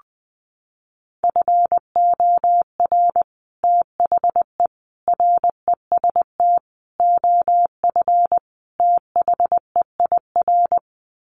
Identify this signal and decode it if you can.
Morse code, heard as a keyed tone, about 20 wpm, 700 Hz. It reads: FOR THE REST OF THEIR
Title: Sets of 5 Words Encore2 20wpm